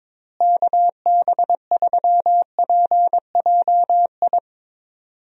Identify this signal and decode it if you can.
Morse code, heard as a keyed tone, about 21 wpm, 700 Hz. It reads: KB3PJI
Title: USA Call Signs - 6 Characters - Repeated 2x - 22wpm